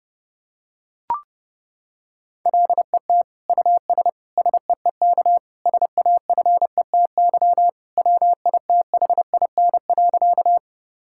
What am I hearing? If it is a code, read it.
Morse code, heard as a keyed tone, about 30 wpm, 700 Hz. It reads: LET US SEEK SAFETY WITHIN.
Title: Sentences from Top 1600 Words 30wpm